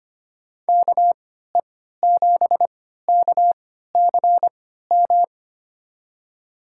Morse code, heard as a keyed tone, about 25 wpm, spaced as for 14 wpm, 700 Hz. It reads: KE7KCM
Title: USA Call Signs - 3x Character Spacing - 25wpm